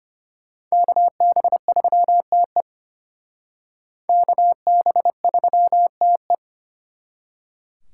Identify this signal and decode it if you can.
Morse code, heard as a keyed tone, about 32 wpm, 700 Hz. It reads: KB3TE KB3TE